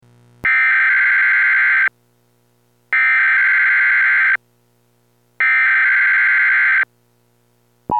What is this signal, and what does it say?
Emergency Alert System (SAME) tones (header bursts), heard as harsh buzzing bursts. It is heard from 0.5 to 6.8 s.
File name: radio-shack / wx-audio / required-monthly-test